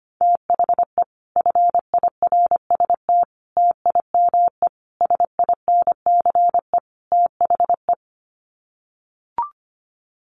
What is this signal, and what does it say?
Morse code, heard as a keyed tone, about 25 wpm, 700 Hz. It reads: THE FIRST TIME SINCE THE
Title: Sets of 5 Words Encore 25wpm